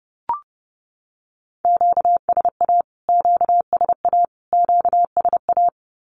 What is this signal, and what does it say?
Morse code, heard as a keyed tone, about 28 wpm, 700 Hz. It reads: QSA QSA QSA
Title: Q-Codes Repeated 3x 30wpm